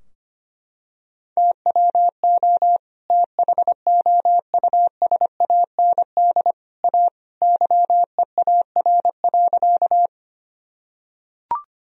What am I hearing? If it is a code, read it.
Morse code, heard as a keyed tone, about 25 wpm, 700 Hz. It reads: TWO THOUSAND A YEAR.
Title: Sentences from Top 1000 Words - Review Part 3 - 25wpm